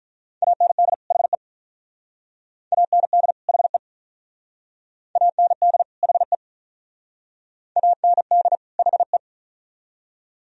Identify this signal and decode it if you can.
Morse code, heard as a keyed tone, about 46 wpm, 700 Hz. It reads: AND HE AND HE AND HE AND HE